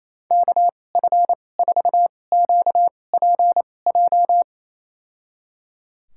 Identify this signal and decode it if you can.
Morse code, heard as a keyed tone, about 28 wpm, spaced as for 20 wpm, 700 Hz. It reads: KF4QPJ
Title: USA Call Signs - 6 Characters - 2x Character Spacing - 28wpm